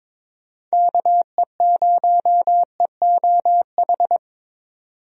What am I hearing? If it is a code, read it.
Morse code, heard as a keyed tone, about 22 wpm, 700 Hz. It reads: KE0EOH
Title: USA Call Signs - 6 Characters - 22wpm